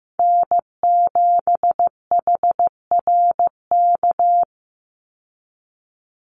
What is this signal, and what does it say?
Morse code, heard as a keyed tone, about 15 wpm, 700 Hz. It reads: N7HRK
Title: Call Signs Repeated 3x - 15wpm